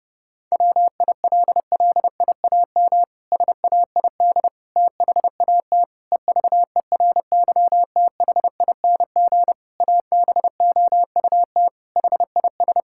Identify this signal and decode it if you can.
Morse code, heard as a keyed tone, about 28 wpm, 700 Hz. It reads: WILLIAM SAID THAT EVERYTHING ABOUT HIS
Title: Pangram Challenge v2 - 30wpm